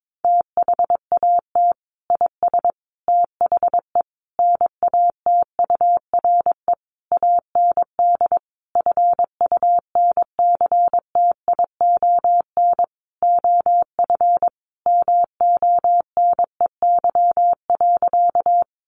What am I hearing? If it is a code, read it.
Morse code, heard as a keyed tone, about 22 wpm, 700 Hz. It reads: THAT IS THE NATURE AND FUNCTION OF MONEY.
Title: Sentences from Top 1100 Words 22wpm